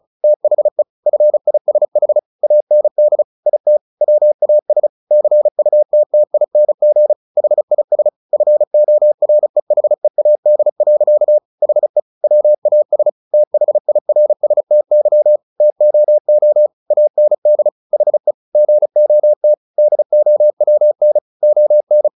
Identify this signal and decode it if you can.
Morse code, heard as a keyed tone, about 34 wpm, 600 Hz. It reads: THE FISH AND IT WAS CUTTING HIS FOREHEAD. HE WAS THIRSTY TOO AND HE GOT DOWN ON
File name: the old man and the sea / the old man and the sea 1 35wpm